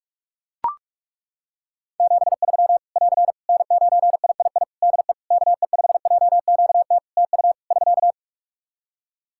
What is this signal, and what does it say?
Morse code, heard as a keyed tone, about 41 wpm, 700 Hz. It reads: 73 <AR> N9III DE KE5JQT TU <SK>
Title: Practice QSO - Exchange 4 - Ending - 45wpm